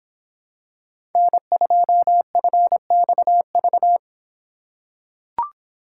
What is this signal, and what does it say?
Morse code, heard as a keyed tone, about 24 wpm, 700 Hz. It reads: N2FXV